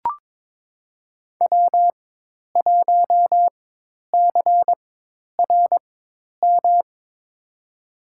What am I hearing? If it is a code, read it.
Morse code, heard as a keyed tone, about 21 wpm, spaced as for 10 wpm, 700 Hz. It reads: W1CRM
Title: USA Call Signs - 4x Character Spacing - 22wpm